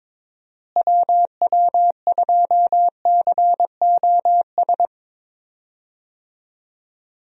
Morse code, heard as a keyed tone, about 22 wpm, 700 Hz. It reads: WW2COS
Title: USA Call Signs - 6 Characters - Repeated 3x - 22wpm